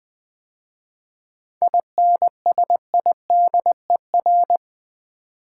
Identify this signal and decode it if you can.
Morse code, heard as a keyed tone, about 20 wpm, 700 Hz. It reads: INSIDER
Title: 7 Letter Words - First 3 Letters Given - 20wpm